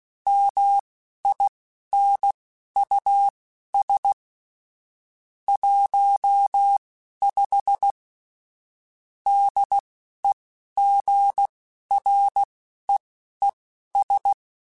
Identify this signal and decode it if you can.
Morse code, heard as a keyed tone, about 16 wpm, spaced as for 11 wpm, 800 Hz. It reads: MINUS 15 DEGREES
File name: QSOs / QSOs-mp3s / WD0BGP 2@10